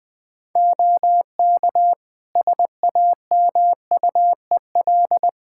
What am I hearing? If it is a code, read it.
Morse code, heard as a keyed tone, about 21 wpm, 700 Hz. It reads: OK SAMUEL
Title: K1USN SST QSO Practice 20wpm